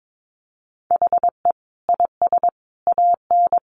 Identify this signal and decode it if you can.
Morse code, heard as a keyed tone, about 22 wpm, 700 Hz. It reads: HE IS AN
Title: Sentences from Top 900 Words 22wpm